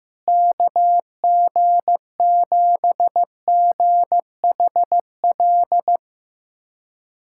Morse code, heard as a keyed tone, about 15 wpm, 700 Hz. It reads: KG7GHL
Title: USA Callsign DE Callsign - 6 Characters - 15wpm